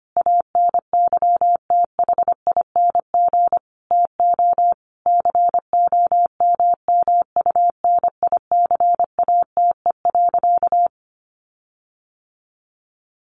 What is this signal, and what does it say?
Morse code, heard as a keyed tone, about 25 wpm, 700 Hz. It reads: ANYTHING TO COMMUNICATE.
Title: Sentences from Top 1300 Words 25wpm